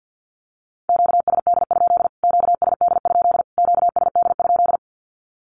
Morse code, heard as a keyed tone, about 50 wpm, 700 Hz. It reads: QSD? QSD? QSD?
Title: Q-Codes Repeated 3x 50wpm